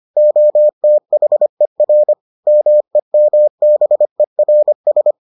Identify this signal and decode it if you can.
Morse code, heard as a keyed tone, about 25 wpm, 600 Hz. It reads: OTHER MEMBERS